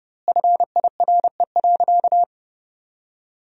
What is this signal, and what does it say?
Morse code, heard as a keyed tone, about 32 wpm, 700 Hz. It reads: FIRE.
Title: 2x Word-Spacing - Sentences from Top 400 Words - 30wpm